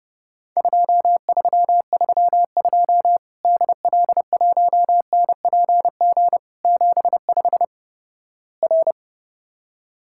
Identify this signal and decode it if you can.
Morse code, heard as a keyed tone, about 32 wpm, 700 Hz. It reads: 2332 DL1NPG 75 R
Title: Worked All Europe DX Contest - QTC Exchange - 30wpm